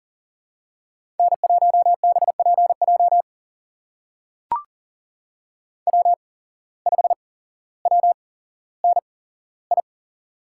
Morse code, heard as a keyed tone, about 40 wpm, spaced as for 17 wpm, 700 Hz. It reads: N1BPJ W5WNI
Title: USA Call Signs - 5 Characters - 8x Character Spacing - 40wpm